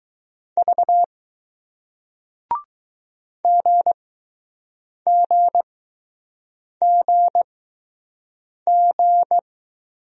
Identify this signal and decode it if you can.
Morse code, heard as a keyed tone, about 20 wpm, spaced as for 6 wpm, 700 Hz. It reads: V GGGG